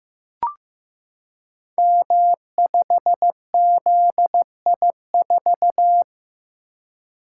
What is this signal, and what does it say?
Morse code, heard as a keyed tone, about 15 wpm, 700 Hz. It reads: M5ZI4
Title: Code Groups - 5 Letter:Numbers - 15wpm